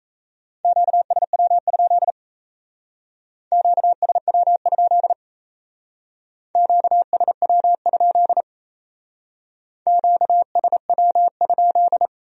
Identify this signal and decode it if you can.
Morse code, heard as a keyed tone, about 34 wpm, 700 Hz. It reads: QSW? QSW? QSW? QSW?
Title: Speed-Racing Q-Codes - 28wpm